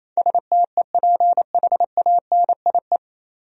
Morse code, heard as a keyed tone, about 28 wpm, 700 Hz. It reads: STEPHANIE
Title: USA Common Names - 28wpm